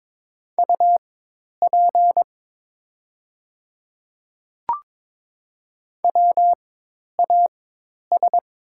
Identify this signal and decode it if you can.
Morse code, heard as a keyed tone, about 22 wpm, spaced as for 10 wpm, 700 Hz. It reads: UP WAS
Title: 4x Character Spacing - Top 200 Words - 22wpm